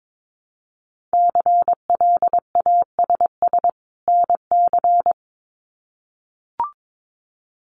Morse code, heard as a keyed tone, about 22 wpm, 700 Hz. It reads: CLASS NC
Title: Winter Field Day Exchanges 22wpm